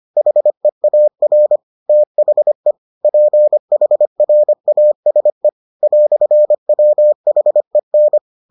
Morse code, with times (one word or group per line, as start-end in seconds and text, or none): 0.2-1.6 HEAR
1.9-2.7 THE
3.0-5.5 PHRASE
5.8-8.2 "WHEN